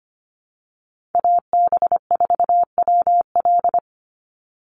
Morse code, heard as a keyed tone, about 25 wpm, 700 Hz. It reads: AB4WL
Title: NA SOTA Chaser Callsigns 25wpm